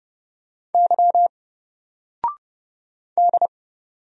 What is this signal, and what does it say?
Morse code, heard as a keyed tone, about 30 wpm, 700 Hz. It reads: Y D
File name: Single-Letter-Number-30wpm